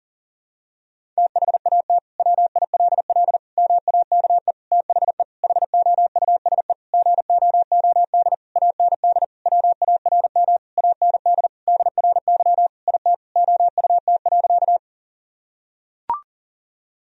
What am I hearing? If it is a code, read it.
Morse code, heard as a keyed tone, about 40 wpm, 700 Hz. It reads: THAT WILL MAKE THE HOUSE GOOD AND WARM AND DRY IT OUT.
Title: Sentences from Top 1000 Words - Review Part 3 - 40wpm